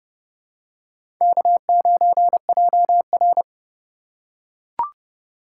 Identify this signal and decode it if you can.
Morse code, heard as a keyed tone, about 30 wpm, 700 Hz. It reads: K9JR
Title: USA Call Signs - 4 Characters - 2x Character Spacing - 30wpm